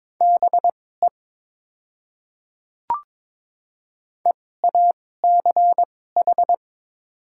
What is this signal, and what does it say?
Morse code, heard as a keyed tone, about 22 wpm, spaced as for 16 wpm, 700 Hz. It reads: BE EACH
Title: 2x Character Spacing - Top 500 Words - 22wpm